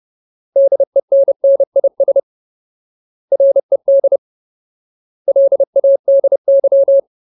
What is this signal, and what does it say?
Morse code, heard as a keyed tone, about 30 wpm, 550 Hz. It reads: DENNIS RED LADY
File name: Ultimate Word List / 30WPM Ultimate